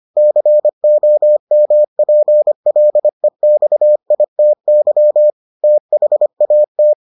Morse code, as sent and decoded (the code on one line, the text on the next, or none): -.-. --- -- .--. .-.. -..- .. - -.-- / - .... .- -
COMPLEXITY THAT